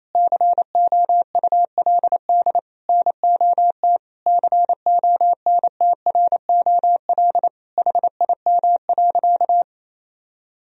Morse code, heard as a keyed tone, about 28 wpm, 700 Hz. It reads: COULD NOT CONTROL HIM.
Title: Sentences from Top 1000 Words - Review Part 3 - 28wpm